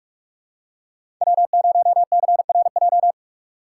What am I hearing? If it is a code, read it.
Morse code, heard as a keyed tone, about 47 wpm, 700 Hz. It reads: W0CRJ
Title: USA Call Signs - 4x Character Spacing - 45wpm